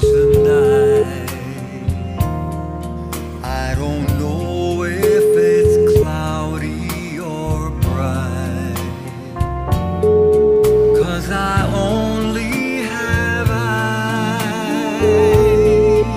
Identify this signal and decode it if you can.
Ringback tone (Europe, 425 Hz), with noise.